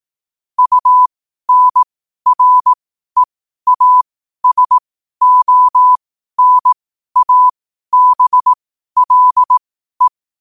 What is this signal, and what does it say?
Morse code, heard as a keyed tone, about 18 wpm, spaced as for 12 wpm, 1000 Hz. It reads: UNREASONABLE
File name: qotdfiles / 12wpm